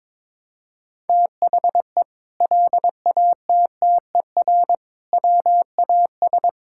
Morse code, heard as a keyed tone, about 22 wpm, 700 Hz. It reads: THE LATTER WAS
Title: Sentences from Top 1600 Words 22wpm